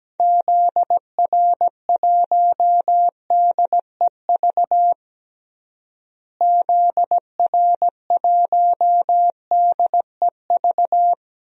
Morse code, heard as a keyed tone, about 17 wpm, 700 Hz. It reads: ZR1DEV ZR1DEV
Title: DXCC-100 Call Signs - 6 Characters - Repeated 3x - 17wpm